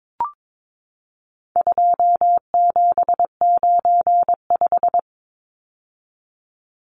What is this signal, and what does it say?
Morse code, heard as a keyed tone, about 22 wpm, 700 Hz. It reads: 2795
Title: Numbers - 4-Digit 22wpm